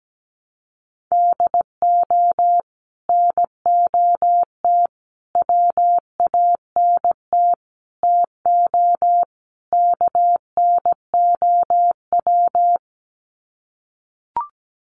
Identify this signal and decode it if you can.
Morse code, heard as a keyed tone, about 17 wpm, 700 Hz. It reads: DO NOT WANT TO KNOW
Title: Sets of 5 Words Encore 17wpm